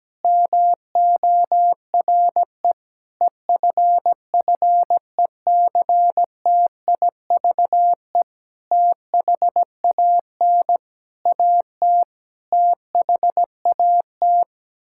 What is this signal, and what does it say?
Morse code, heard as a keyed tone, about 17 wpm, 700 Hz. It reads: MORE EFFECTIVE THAN AT THAT
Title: Sentences from Top 1000 Words - Review Part 6 - 17wpm